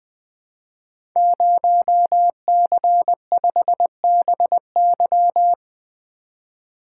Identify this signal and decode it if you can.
Morse code, heard as a keyed tone, about 20 wpm, 700 Hz. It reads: 0C5BY
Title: Code Groups - 5 Letter:Numbers - 20wpm